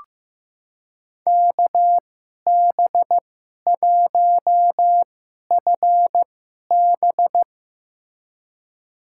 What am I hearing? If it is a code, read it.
Morse code, heard as a keyed tone, about 15 wpm, spaced as for 11 wpm, 700 Hz. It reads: KB1FB
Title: USA Call Signs - 5 Characters - 2x Character Spacing - 15wpm